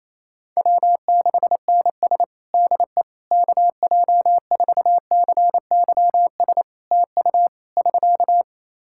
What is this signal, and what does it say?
Morse code, heard as a keyed tone, about 28 wpm, 700 Hz. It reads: W6NS DE KJ4CYS TU <SK>
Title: Practice QSO - Exchange 4 - Ending - 28wpm